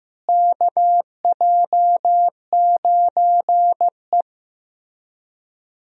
Morse code, heard as a keyed tone, about 16 wpm, 700 Hz. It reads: KJ9E
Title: USA Call Signs - 4 Characters - Repeated 3x - 15wpm